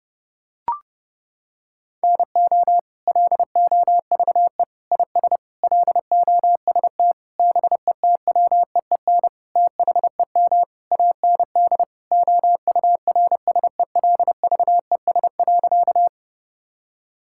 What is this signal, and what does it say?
Morse code, heard as a keyed tone, about 30 wpm, 700 Hz. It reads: NO LOVE IS LOST BETWEEN THEM AND OURSELVES.
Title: Sentences from Top 1600 Words 30wpm